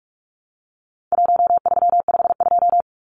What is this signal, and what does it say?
Morse code, heard as a keyed tone, about 44 wpm, 700 Hz. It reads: 1352
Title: Numbers - Contest Style - 4-Digit 45wpm